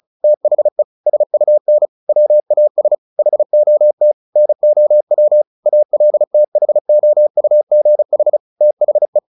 Morse code, heard as a keyed tone, about 35 wpm, 600 Hz. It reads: THE SUN WAS HOT NOW ALTHOUGH THE
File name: the old man and the sea / the old man and the sea 1 35wpm